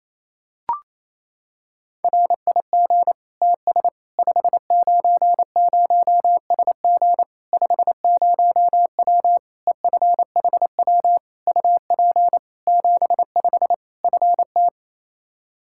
Morse code, heard as a keyed tone, about 28 wpm, 700 Hz. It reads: RIG TS 590SG 50W EFHW UP 75 FT
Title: Practice QSO - Exchange 2 - Info - 28wpm